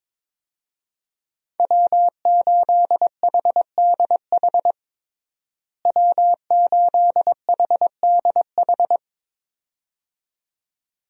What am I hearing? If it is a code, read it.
Morse code, heard as a keyed tone, about 22 wpm, 700 Hz. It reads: W8HDH W8HDH
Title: USA Call Signs - 5 Characters - Repeated 2x - 22wpm